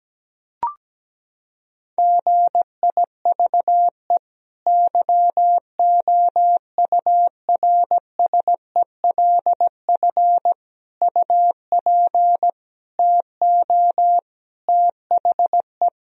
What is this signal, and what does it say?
Morse code, heard as a keyed tone, about 17 wpm, 700 Hz. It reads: GIVE YOURSELF UP TO THE
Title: Sentences from Top 1000 Words - Review Part 4 - 17wpm